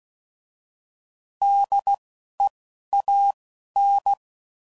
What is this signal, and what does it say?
Morse code, heard as a keyed tone, about 16 wpm, spaced as for 11 wpm, 800 Hz. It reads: DEAN
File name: Prefixes / By First Letter / Ds